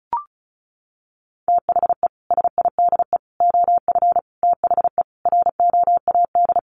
Morse code, heard as a keyed tone, about 34 wpm, 700 Hz. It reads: THE SIDE OF THE ROAD
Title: Sets of 5 Words 35wpm